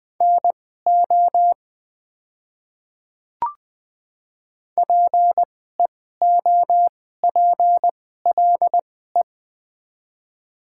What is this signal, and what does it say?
Morse code, heard as a keyed tone, about 20 wpm, spaced as for 14 wpm, 700 Hz. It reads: NO PEOPLE
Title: 2x Character Spacing - Top 100 Words - 20wpm